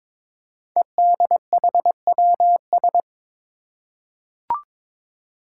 Morse code, heard as a keyed tone, about 22 wpm, 700 Hz. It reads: EDHWS
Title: Five Challenging Letter-Number Combinations 22wpm